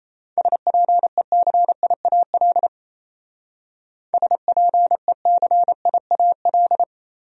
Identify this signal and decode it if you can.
Morse code, heard as a keyed tone, about 36 wpm, 700 Hz. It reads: SPECIAL SPECIAL